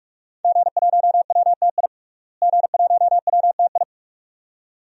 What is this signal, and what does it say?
Morse code, heard as a keyed tone, about 46 wpm, 700 Hz. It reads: G1WTI G1WTI
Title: DXCC-100 Call Signs - 5 Characters - Repeated 2x - 45wpm